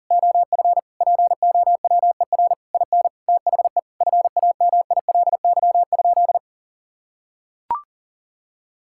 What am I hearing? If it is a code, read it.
Morse code, heard as a keyed tone, about 40 wpm, 700 Hz. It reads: OF POWER IN THE FAMILY?
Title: Sentences from Top 1000 Words - Review Part 6 - 40wpm